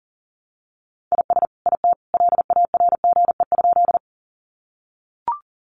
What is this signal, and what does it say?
Morse code, heard as a keyed tone, about 40 wpm, 700 Hz. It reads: IS IT LARGE?
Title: Sentences from Top 200 Words 40wpm